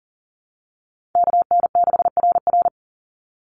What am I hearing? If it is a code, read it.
Morse code, heard as a keyed tone, about 40 wpm, 700 Hz. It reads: KN6RR
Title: USA Call Signs 40wpm